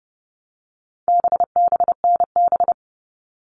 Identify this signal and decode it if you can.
Morse code, heard as a keyed tone, about 30 wpm, 700 Hz. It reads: BBNB
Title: All Cut Numbers - 4-Digit 30wpm